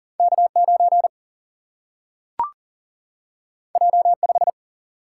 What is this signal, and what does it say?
Morse code, heard as a keyed tone, about 39 wpm, 700 Hz. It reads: K 9 J 5
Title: Single-Letter Single-Number 40wpm